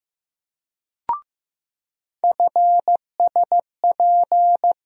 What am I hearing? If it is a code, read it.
Morse code, heard as a keyed tone, about 15 wpm, 700 Hz. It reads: FSP
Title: USA Call Sign Postfixes - 3 Letters 15wpm